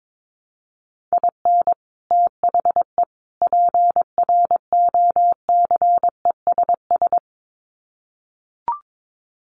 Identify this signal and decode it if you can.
Morse code, heard as a keyed tone, about 22 wpm, 700 Hz. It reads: IN THE PROCESS
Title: Sets of 3 Words 22wpm